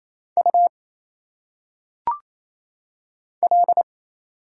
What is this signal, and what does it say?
Morse code, heard as a keyed tone, about 27 wpm, 700 Hz. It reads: U L
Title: Single Letters - Phonetic Pronunciation 28wpm